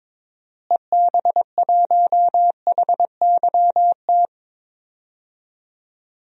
Morse code, heard as a keyed tone, about 22 wpm, 700 Hz. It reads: EB1HYT
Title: DXCC-100 Call Signs - 6 Characters - Repeated 3x - 22wpm